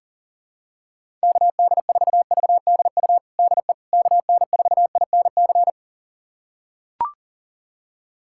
Morse code, heard as a keyed tone, about 40 wpm, 700 Hz. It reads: KD4VDU DE KN4INC
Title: USA Callsign DE Callsign - 6 Characters - 40wpm